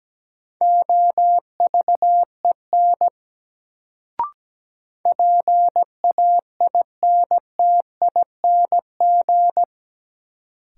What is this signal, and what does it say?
Morse code, heard as a keyed tone, about 17 wpm, 700 Hz. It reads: OVEN PAINTING
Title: Sentences from Top 1200 Words - New Word Drill Alt - 17wpm